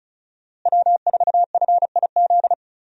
Morse code, heard as a keyed tone, about 34 wpm, 700 Hz. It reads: W4FIZ
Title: USA Call Signs - 5 Characters - Repeated 2x - 35wpm